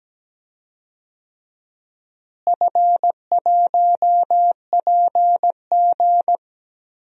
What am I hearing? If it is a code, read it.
Morse code, heard as a keyed tone, about 17 wpm, 700 Hz. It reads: F1PG
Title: DXCC-100 Call Signs - 4 Characters - Repeated 2x - 17wpm